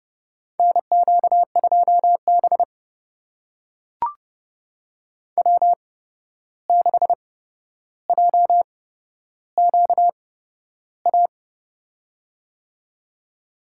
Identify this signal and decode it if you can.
Morse code, heard as a keyed tone, about 30 wpm, spaced as for 12 wpm, 700 Hz. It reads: NQ2B W6JQA
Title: USA Call Signs - 8x Character Spacing - 30wpm